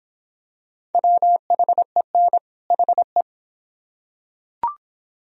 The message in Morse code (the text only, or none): WHEN HE